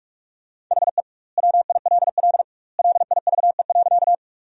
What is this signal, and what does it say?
Morse code, heard as a keyed tone, about 44 wpm, 700 Hz. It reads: HE WILL LIVE.
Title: 2x Word-Spacing - Sentences from Top 200 Words - 45wpm